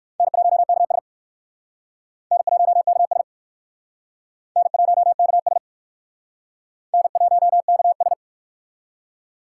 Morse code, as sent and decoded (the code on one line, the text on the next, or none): -. .---- -.- ... / -. .---- -.- ... / -. .---- -.- ... / -. .---- -.- ...
N1KS N1KS N1KS N1KS